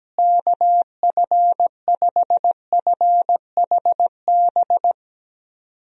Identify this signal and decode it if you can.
Morse code, heard as a keyed tone, about 17 wpm, 700 Hz. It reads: KF5FHB
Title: USA Call Signs - 6 Characters - Repeated 3x - 17wpm